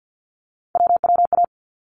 Morse code, heard as a keyed tone, about 50 wpm, 700 Hz. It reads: WWA
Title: ARRL Sections 50wpm